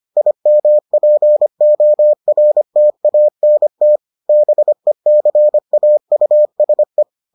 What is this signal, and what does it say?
Morse code, heard as a keyed tone, about 25 wpm, 600 Hz. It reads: IMPORTANT BECAUSE